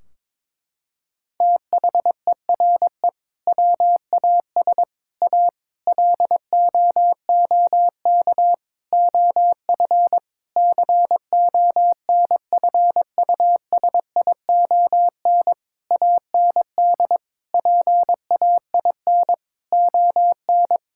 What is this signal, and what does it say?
Morse code, heard as a keyed tone, about 22 wpm, 700 Hz. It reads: THERE WAS A LOOK OF CONFUSION AND PAIN ON
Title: Sentences from Top 1600 Words 22wpm